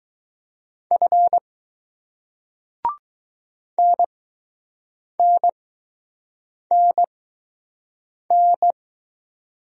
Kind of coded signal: Morse code

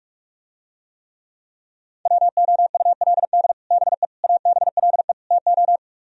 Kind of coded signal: Morse code

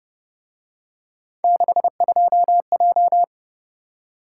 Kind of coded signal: Morse code

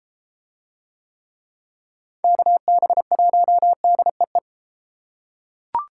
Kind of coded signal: Morse code

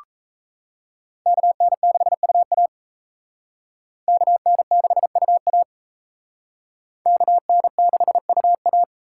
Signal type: Morse code

